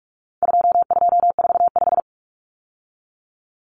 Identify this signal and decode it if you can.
Morse code, heard as a keyed tone, about 44 wpm, 700 Hz. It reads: 2245
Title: Numbers - 4-Digit 45wpm